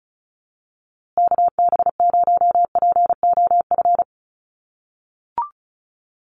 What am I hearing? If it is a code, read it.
Morse code, heard as a keyed tone, about 34 wpm, 700 Hz. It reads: KB0POF
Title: USA Call Signs Repeated 2x 35wpm